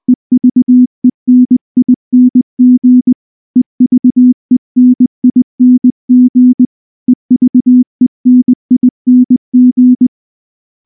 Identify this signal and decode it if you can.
Morse code, heard as a keyed tone, about 19 wpm, 250 Hz. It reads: EVENING EVENING EVENING
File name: morse / morse-training-20WPM-10